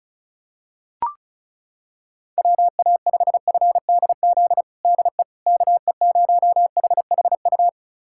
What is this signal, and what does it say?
Morse code, heard as a keyed tone, about 36 wpm, 700 Hz. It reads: WA5FDZ DE KE0HHU
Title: USA Callsign DE Callsign 35wpm